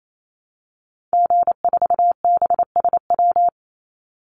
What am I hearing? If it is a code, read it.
Morse code, heard as a keyed tone, about 28 wpm, 700 Hz. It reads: G4BSW
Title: Call Signs Repeated 3x - 28wpm